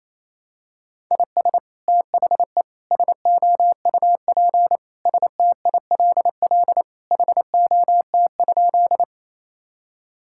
Morse code, heard as a keyed tone, about 28 wpm, 700 Hz. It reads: IS THE SOUP STILL HOT?
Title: Sentences from Top 1000 Words - Review Part 6 - 28wpm